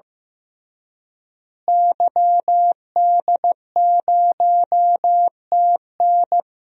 Morse code, heard as a keyed tone, about 16 wpm, 700 Hz. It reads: YD0TN
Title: DXCC-100 Call Signs - 5 Characters - Repeated 3x - 15wpm